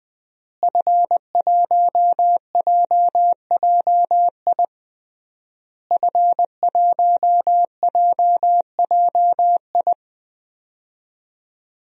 Morse code, heard as a keyed tone, about 20 wpm, 700 Hz. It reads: F1JJI F1JJI
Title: DXCC-100 Call Signs - 5 Characters - Repeated 3x - 20wpm